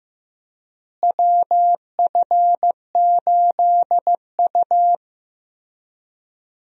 Morse code, heard as a keyed tone, about 15 wpm, 700 Hz. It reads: WF8U
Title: USA Call Signs - 4 Characters - Repeated 2x - 15wpm